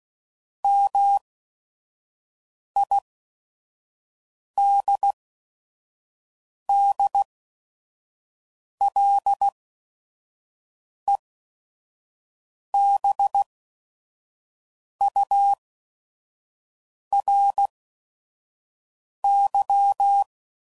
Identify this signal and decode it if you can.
Morse code, heard as a keyed tone, about 16 wpm, spaced as for 5 wpm, 800 Hz. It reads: MIDDLEBURY
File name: CW Code Groups @5WPM / fromSM's / Cities